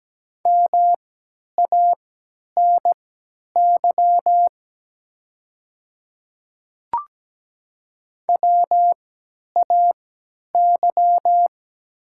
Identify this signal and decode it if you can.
Morse code, heard as a keyed tone, about 17 wpm, spaced as for 10 wpm, 690 Hz. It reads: MANY WAY
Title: 3x Character Spacing - Top 100 Words - 17wpm